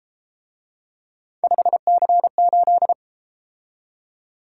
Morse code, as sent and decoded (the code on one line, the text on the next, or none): ..... -.-. ---..
5C8